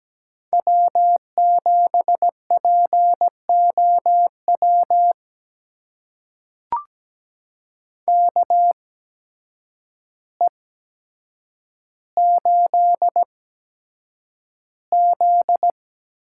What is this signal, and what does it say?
Morse code, heard as a keyed tone, about 17 wpm, spaced as for 8 wpm, 700 Hz. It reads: W7POW KE8Z
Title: USA Call Signs - 8x Character Spacing - 17wpm